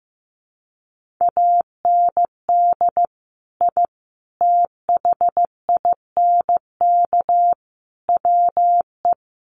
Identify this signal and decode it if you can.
Morse code, heard as a keyed tone, about 15 wpm, 700 Hz. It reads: AND I THINK WE
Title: Sets of 4 Words 15wpm